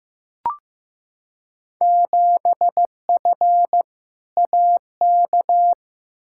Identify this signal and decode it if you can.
Morse code, heard as a keyed tone, about 15 wpm, 700 Hz. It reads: 7F AK
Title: Field Day Exchanges Encore - 15wpm